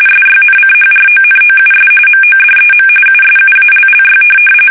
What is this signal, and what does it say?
RTTY, about 45.45 baud. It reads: SYNCHRONOUS MODEM USUALLY
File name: fsk850 / teletype-corp